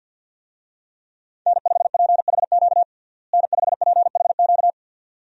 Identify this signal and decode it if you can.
Morse code, heard as a keyed tone, about 50 wpm, 700 Hz. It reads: N5PHQ N5PHQ
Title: USA Call Signs - 5 Characters - Repeated 2x - 50wpm